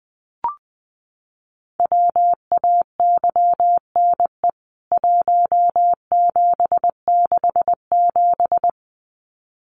Morse code, heard as a keyed tone, about 20 wpm, 700 Hz. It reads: WAYNE 1767
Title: CWT Exchange 20wpm